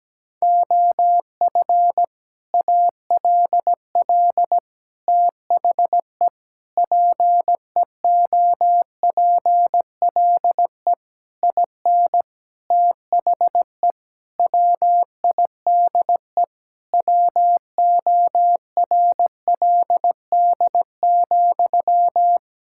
Morse code, heard as a keyed tone, about 17 wpm, 700 Hz. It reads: OF ALL THE PEOPLE IN THE WIDE WORLD,
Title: Sentences from Top 1000 Words - Review Part 6 - 17wpm